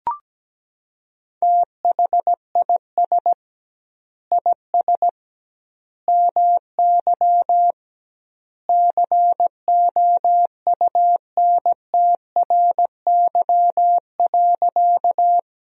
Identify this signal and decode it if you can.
Morse code, heard as a keyed tone, about 17 wpm, 700 Hz. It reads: THIS IS MY COUNTRY.
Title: 2x Word-Spacing - Sentences from Top 300 Words - 17wpm